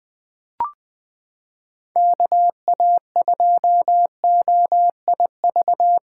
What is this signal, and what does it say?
Morse code, heard as a keyed tone, about 21 wpm, 700 Hz. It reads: KA2OIV
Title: USA Callsign DE Callsign - 6 Characters - 20wpm